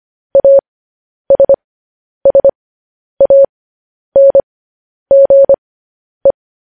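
Morse code, heard as a keyed tone, about 25 wpm, spaced as for 10 wpm, 550 Hz. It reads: ASSANGE
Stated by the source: Morse code practice of news headlines at 10 wpm.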